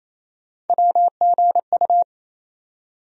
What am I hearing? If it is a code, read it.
Morse code, heard as a keyed tone, about 28 wpm, 700 Hz. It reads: WGU
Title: USA Call Sign Postfixes - 3 Letters 28wpm